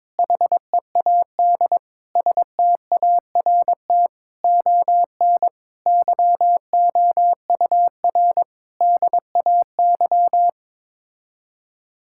Morse code, heard as a keyed tone, about 22 wpm, 700 Hz. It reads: HEAD START ON YOUR DAY
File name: Sets of 9 Words - 22wpm